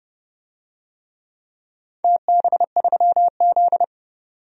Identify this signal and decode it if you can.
Morse code, heard as a keyed tone, about 31 wpm, 700 Hz. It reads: TB3Z
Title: DXCC-100 Call Signs - 4 Characters - 30wpm